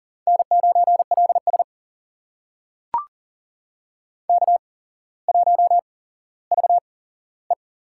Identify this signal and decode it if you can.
Morse code, heard as a keyed tone, about 42 wpm, spaced as for 17 wpm, 700 Hz. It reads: N9LS K1VE
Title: USA Call Signs - 4 Characters - 8x Character Spacing - 40wpm